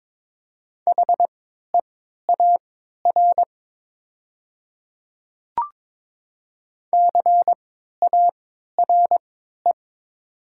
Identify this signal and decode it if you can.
Morse code, heard as a keyed tone, about 22 wpm, spaced as for 12 wpm, 700 Hz. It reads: HEAR CARE
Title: 3x Character Spacing - Top 300 Words - 22wpm